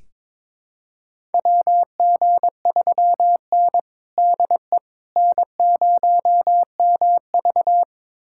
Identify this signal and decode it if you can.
Morse code, heard as a keyed tone, about 21 wpm, 700 Hz. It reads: WG3N DE N0MV
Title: USA Callsign DE Callsign - 4 Characters - 22wpm